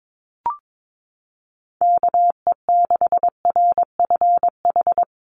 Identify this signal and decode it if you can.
Morse code, heard as a keyed tone, about 22 wpm, 700 Hz. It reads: KE6RFH
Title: USA Call Signs Repeated 3x 22wpm